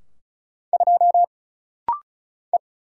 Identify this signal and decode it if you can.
Morse code, heard as a keyed tone, about 34 wpm, 700 Hz. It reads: 2 E
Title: Single Letter-Number Alt - Rapid-Fire 35wpm